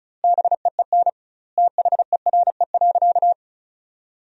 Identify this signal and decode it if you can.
Morse code, heard as a keyed tone, about 34 wpm, 700 Hz. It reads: BEEN THERE.
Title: 2x Word-Spacing - Sentences from Top 300 Words - 35wpm